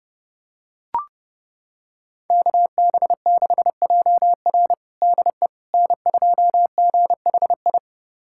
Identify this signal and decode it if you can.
Morse code, heard as a keyed tone, about 30 wpm, 700 Hz. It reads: KB6JR DE N2GHI
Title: USA Callsign DE Callsign 30wpm